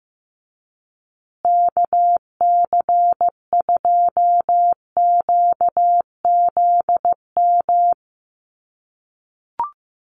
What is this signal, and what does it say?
Morse code, heard as a keyed tone, about 15 wpm, 700 Hz. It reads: KC2QZM
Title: USA Call Signs Repeated 2x 15wpm